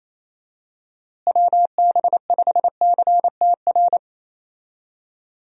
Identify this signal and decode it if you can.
Morse code, heard as a keyed tone, about 28 wpm, 700 Hz. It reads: WB5CTR
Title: USA Call Signs - 6 Characters - Repeated 2x - 28wpm